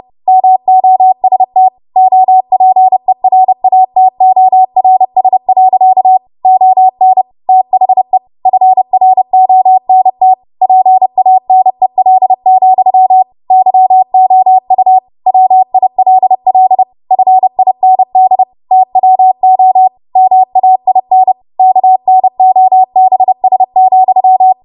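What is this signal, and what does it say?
Morse code, heard as a keyed tone, about 30 wpm, 750 Hz. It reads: MOST OPERATORS. ON THE FRONT PANEL, YOU WILL FIND TWO MAIN KNOBS,
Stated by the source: Morse practice transmission at 30 WPM.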